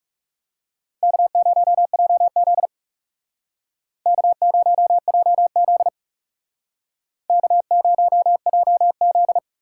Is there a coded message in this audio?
Yes: Morse code from 1.0 to 9.4 s, heard as a keyed tone, about 40 wpm, 700 Hz. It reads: K0JZ K0JZ K0JZ